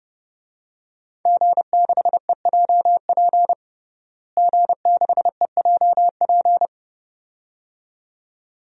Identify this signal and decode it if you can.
Morse code, heard as a keyed tone, about 30 wpm, 700 Hz. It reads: G6EJP G6EJP
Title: DXCC-100 Call Signs - 5 Characters - Repeated 2x - 30wpm